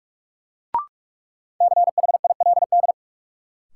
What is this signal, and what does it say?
Morse code, heard as a keyed tone, about 46 wpm, 700 Hz. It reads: CHILD
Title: Sets of 5 Words Encore - New Word Drill - 45wpm